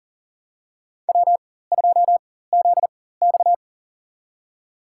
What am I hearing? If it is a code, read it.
Morse code, heard as a keyed tone, about 40 wpm, spaced as for 19 wpm, 700 Hz. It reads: W2ZX
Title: USA Call Signs - 4x Character Spacing - 40wpm